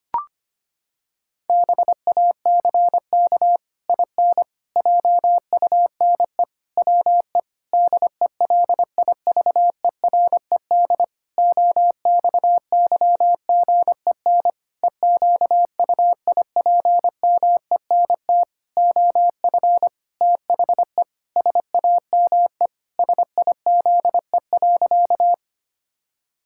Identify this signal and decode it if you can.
Morse code, heard as a keyed tone, about 25 wpm, 700 Hz. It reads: BACK IN JUNE WE DELIVERED OXYGEN EQUIPMENT OF THE SAME SIZE.
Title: Pangram Challenge v2 - 25wpm